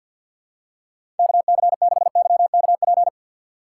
Morse code, heard as a keyed tone, about 53 wpm, 700 Hz. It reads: KC6YKP
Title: USA Call Signs - 4x Character Spacing - 50wpm